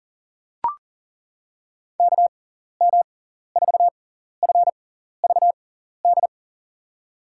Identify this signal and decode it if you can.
Morse code, heard as a keyed tone, about 40 wpm, spaced as for 14 wpm, 700 Hz. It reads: KM4FVD
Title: USA Call Signs - 6 Characters - 6x Character Spacing - 40wpm